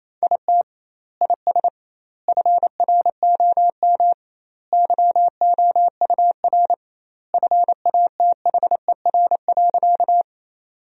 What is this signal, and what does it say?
Morse code, heard as a keyed tone, about 28 wpm, 700 Hz. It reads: IT IS FROM YOUR FATHER.
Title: 2x Word-Spacing - Sentences from Top 300 Words - 28wpm